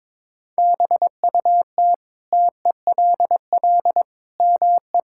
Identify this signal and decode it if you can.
Morse code, heard as a keyed tone, about 22 wpm, 700 Hz. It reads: BUT TELL ME
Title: Sentences from Top 1000 Words - Review Part 6 - 22wpm